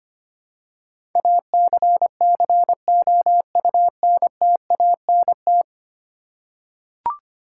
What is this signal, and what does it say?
Morse code, heard as a keyed tone, about 25 wpm, 700 Hz. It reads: ACCOUNTANT
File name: Ten Letter Words - 25wpm